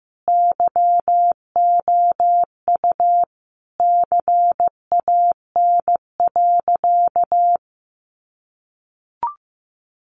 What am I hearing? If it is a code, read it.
Morse code, heard as a keyed tone, about 15 wpm, 700 Hz. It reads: YOU CAN.
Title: Sentences from Top 1200 Words 15wpm